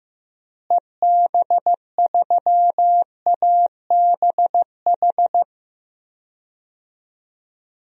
Morse code, heard as a keyed tone, about 15 wpm, 700 Hz. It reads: EB3ABH
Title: DXCC-100 Call Signs - 6 Characters - Repeated 3x - 15wpm